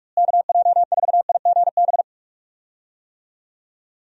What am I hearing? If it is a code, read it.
Morse code, heard as a keyed tone, about 44 wpm, 700 Hz. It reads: KJ4IGB
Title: USA Call Signs - 6 Characters - Repeated 3x - 45wpm